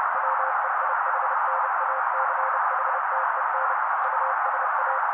Morse code, heard as a keyed tone, about 30 wpm, 530 Hz. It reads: WEIHNACHTEN USA
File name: daily / 29wpm noise